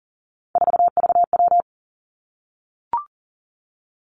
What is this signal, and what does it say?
Morse code, heard as a keyed tone, about 40 wpm, 700 Hz. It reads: 4VW
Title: Three Challenging Letter-Number Combinations 40wpm